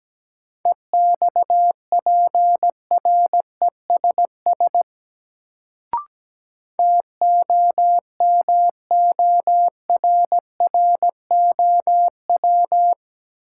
Morse code, heard as a keyed tone, about 17 wpm, 700 Hz. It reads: EXPRESS TOMORROW
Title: Sentences from Top 1100 Words - New Word Drill - 17wpm